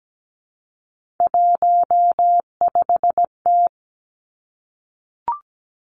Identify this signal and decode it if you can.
Morse code, heard as a keyed tone, about 17 wpm, 700 Hz. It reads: 15T
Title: Three Challenging Letter-Number Combinations 17wpm